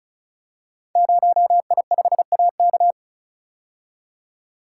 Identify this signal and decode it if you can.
Morse code, heard as a keyed tone, about 36 wpm, 700 Hz. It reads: 0I5AK
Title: Code Groups - 5 Letter:Numbers - 35wpm